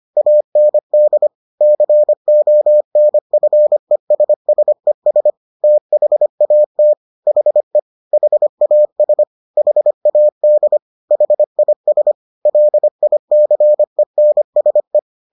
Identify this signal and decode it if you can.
Morse code, heard as a keyed tone, about 25 wpm, 600 Hz. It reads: AND CONFESSES THAT HE HAS HAD HIS LICENSE